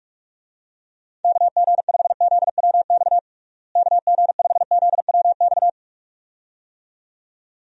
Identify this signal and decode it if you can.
Morse code, heard as a keyed tone, about 42 wpm, 700 Hz. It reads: KG5ZWX KG5ZWX
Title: USA Call Signs - 6 Characters - Repeated 2x - 45wpm